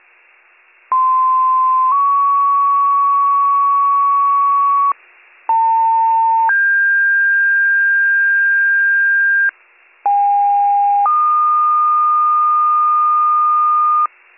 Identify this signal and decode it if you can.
Two-tone page, heard as a one-second tone, then a three-second tone, about 1040 then 1120 Hz, about 900 then 1600 Hz, about 810 then 1220 Hz. It is heard from 0.9 to 14.1 s.